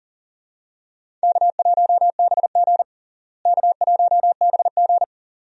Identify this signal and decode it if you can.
Morse code, heard as a keyed tone, about 40 wpm, 700 Hz. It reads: K1BG K1BG
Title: USA Call Signs - 4 Characters - Repeated 2x - 40wpm